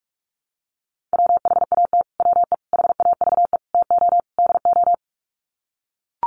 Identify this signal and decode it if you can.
Morse code, heard as a keyed tone, about 44 wpm, 700 Hz. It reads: WHAT WE HAVE TO DO
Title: Sets of 5 Words Encore 45wpm